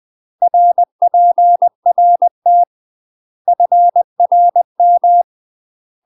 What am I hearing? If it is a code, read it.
Morse code, heard as a keyed tone, about 20 wpm, 700 Hz. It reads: RPRT FRM
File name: UK MASTER QSOs TRAINING/20WPM